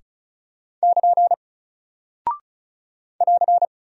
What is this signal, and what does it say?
Morse code, heard as a keyed tone, about 36 wpm, 700 Hz. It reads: <KN> <AR>
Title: Single Character (Letter:Number:Punctuation:Prosign) Alt 35wpm